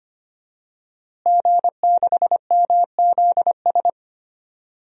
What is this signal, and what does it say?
Morse code, heard as a keyed tone, about 25 wpm, 700 Hz. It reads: G6MZS
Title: DXCC-100 Call Signs - 5 Characters - Repeated 2x - 25wpm